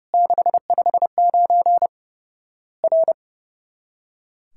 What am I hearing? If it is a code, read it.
Morse code, heard as a keyed tone, about 32 wpm, 700 Hz. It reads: 659 R
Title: Worked All Europe DX Contest - QTC Exchange - 30wpm